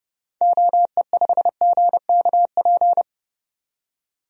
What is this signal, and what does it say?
Morse code, heard as a keyed tone, about 32 wpm, 700 Hz. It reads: OE5GKP
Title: DXCC-100 Call Signs - 6 Characters - 30wpm